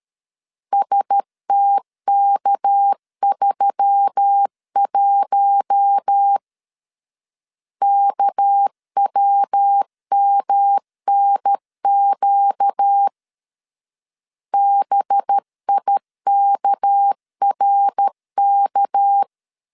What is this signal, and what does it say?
Morse code, heard as a keyed tone, about 13 wpm, 800 Hz. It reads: STK31 KWMNQ BIKRK
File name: LEZIONI / 6 PKX0 / 6 PK 12wpm 2006